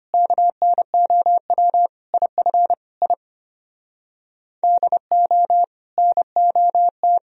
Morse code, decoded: KNOW IF I DO NOT

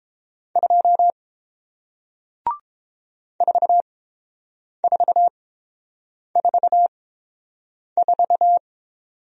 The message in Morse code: 2 4444

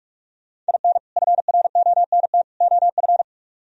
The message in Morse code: IN FRONT OF

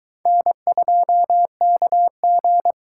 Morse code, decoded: N2KG

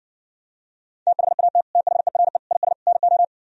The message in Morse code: THAT THERE IS NO

